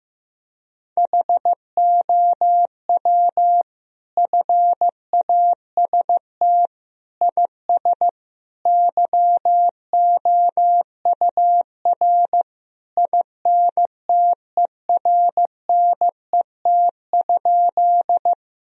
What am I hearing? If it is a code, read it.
Morse code, heard as a keyed tone, about 16 wpm, 700 Hz. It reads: HOW FAST IS YOUR INTERNET?
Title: Sentences from Top 1000 Words - Review Part 4 - 15wpm